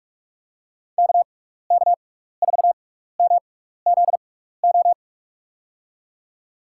Morse code, heard as a keyed tone, about 44 wpm, spaced as for 15 wpm, 700 Hz. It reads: KK4MZO
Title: USA Call Signs - 6 Characters - 6x Character Spacing - 45wpm